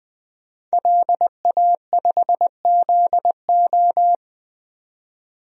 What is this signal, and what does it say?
Morse code, heard as a keyed tone, about 20 wpm, 700 Hz. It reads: LA5ZO
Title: DXCC-100 Call Signs - 5 Characters - 20wpm